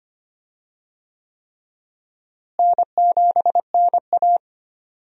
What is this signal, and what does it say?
Morse code, heard as a keyed tone, about 25 wpm, 700 Hz. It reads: N7NA
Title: USA Call Signs - 4 Characters - Repeated 3x - 25wpm